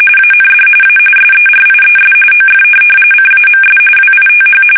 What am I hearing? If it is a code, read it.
RTTY, about 45.45 baud. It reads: AND A SERIES OF CAMS WITH